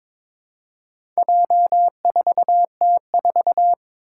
Morse code, heard as a keyed tone, about 22 wpm, 700 Hz. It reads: J4T4
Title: Four Challenging Letter-Number Combinations 22wpm